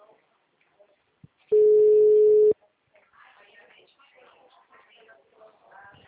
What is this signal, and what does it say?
Ringback tone (Europe, 425 Hz), with speech.
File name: out- -2004- - - .1444